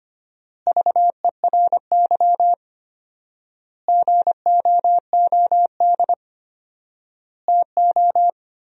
Morse code, heard as a keyed tone, about 25 wpm, 700 Hz. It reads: VERY GOOD TO
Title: 4x Word-Spacing - Sentences from Top 200 Words - 25wpm